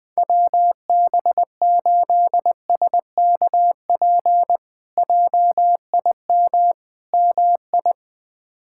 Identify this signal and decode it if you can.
Morse code, heard as a keyed tone, about 21 wpm, 700 Hz. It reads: WB8SKP JIM MI
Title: K1USN SST QSO Practice - 20wpm